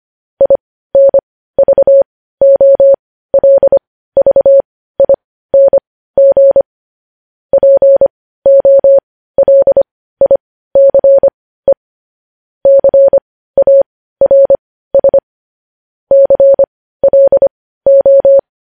Morse code, heard as a keyed tone, about 25 wpm, spaced as for 15 wpm, 550 Hz. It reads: INVOLVING POLICE CARS CLO
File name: podcast / News headlines-1x-25x15